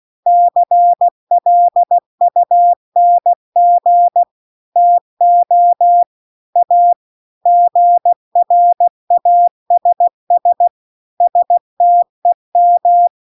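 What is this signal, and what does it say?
Morse code, heard as a keyed tone, about 16 wpm, 700 Hz. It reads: CLUNG TO A GRASS STEM
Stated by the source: Morse practice text at 16 WPM.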